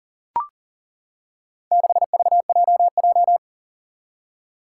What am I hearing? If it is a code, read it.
Morse code, heard as a keyed tone, about 40 wpm, 700 Hz. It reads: 6VJJ
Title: Four Challenging Letter-Number Combinations 40wpm